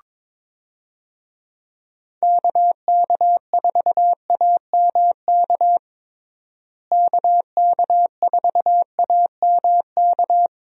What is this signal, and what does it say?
Morse code, heard as a keyed tone, about 21 wpm, 700 Hz. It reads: KK4AMK KK4AMK
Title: USA Call Signs - 6 Characters - Repeated 3x - 22wpm